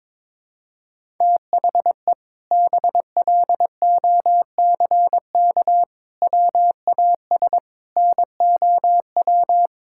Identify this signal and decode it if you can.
Morse code, heard as a keyed tone, about 22 wpm, 700 Hz. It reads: THE BLOCK WAS NOW
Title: Sentences from Top 1600 Words 22wpm